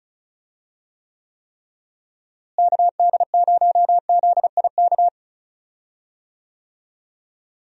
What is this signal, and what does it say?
Morse code, heard as a keyed tone, about 36 wpm, 700 Hz. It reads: KD0ZIK
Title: USA Call Signs - 6 Characters - Repeated 3x - 35wpm